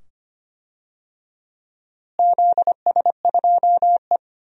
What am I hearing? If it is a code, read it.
Morse code, heard as a keyed tone, about 25 wpm, 700 Hz. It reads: ZS2E
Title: DXCC-100 Call Signs - 4 Characters - Repeated 2x - 25wpm